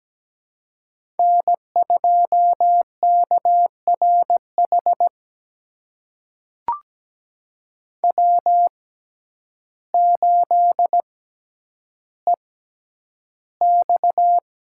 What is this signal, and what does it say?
Morse code, heard as a keyed tone, about 17 wpm, spaced as for 9 wpm, 700 Hz. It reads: N2KRH W8EX
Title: USA Call Signs - 6x Character Spacing - 17wpm